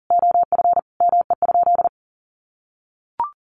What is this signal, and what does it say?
Morse code, heard as a keyed tone, about 37 wpm, 700 Hz. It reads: OF ME?
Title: Sentences from Top 600 Words 40wpm